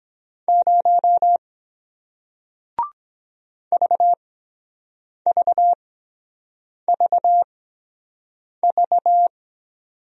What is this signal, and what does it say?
Morse code, heard as a keyed tone, about 24 wpm, spaced as for 7 wpm, 700 Hz. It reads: 0 VVVV